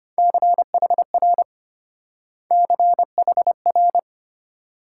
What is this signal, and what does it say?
Morse code, heard as a keyed tone, about 31 wpm, 700 Hz. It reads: CHR CHR
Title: Speed-Racing Multi-letter Phonemes - 25wpm